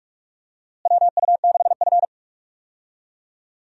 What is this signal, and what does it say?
Morse code, heard as a keyed tone, about 46 wpm, 700 Hz. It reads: WU6F
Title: USA Call Signs - 4 Characters - Repeated 3x - 45wpm